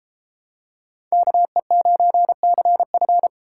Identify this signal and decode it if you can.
Morse code, heard as a keyed tone, about 31 wpm, 700 Hz. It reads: KE9CF